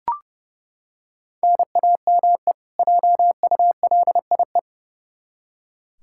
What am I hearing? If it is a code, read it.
Morse code, heard as a keyed tone, about 30 wpm, 700 Hz. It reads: NAME JULIE
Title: K1USN SST QSO Practice 30wpm